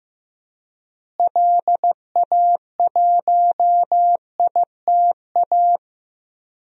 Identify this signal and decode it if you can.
Morse code, heard as a keyed tone, about 15 wpm, 700 Hz. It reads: LA1ITA
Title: DXCC-100 Call Signs - 6 Characters - Repeated 3x - 15wpm